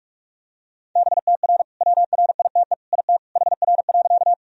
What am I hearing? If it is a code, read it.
Morse code, heard as a keyed tone, about 41 wpm, 700 Hz. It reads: BTR WRITE IT HR.
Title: Sentences from Top 400 Words and Abbreviations 45wpm